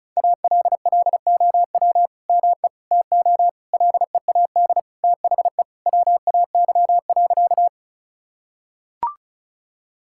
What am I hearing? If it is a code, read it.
Morse code, heard as a keyed tone, about 36 wpm, 700 Hz. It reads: ALLOW ME TO LEAD THE WAY.
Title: Sentences from Top 1000 Words - Review Part 5 - 35wpm